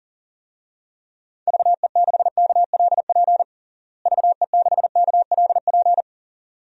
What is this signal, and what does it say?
Morse code, heard as a keyed tone, about 40 wpm, 700 Hz. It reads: VE6KLP VE6KLP
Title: DXCC-100 Call Signs - 6 Characters - Repeated 2x - 40wpm